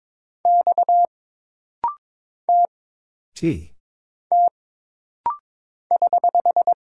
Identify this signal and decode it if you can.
Morse code, heard as a keyed tone, about 22 wpm, 700 Hz. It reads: X T T <HH>
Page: This rapid-fire timing is geared to push your ICR (Instant Character Recognition) skills.